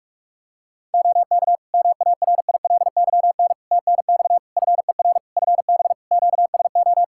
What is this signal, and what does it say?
Morse code, heard as a keyed tone, about 47 wpm, 700 Hz. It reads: OK MARILYN TNX FER FB QSO
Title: Practice QSO - Exchange 4 - Ending - 45wpm